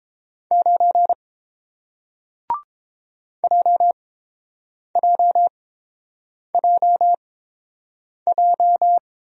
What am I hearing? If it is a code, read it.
Morse code, heard as a keyed tone, about 30 wpm, spaced as for 7 wpm, 700 Hz. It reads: 9 JJJJ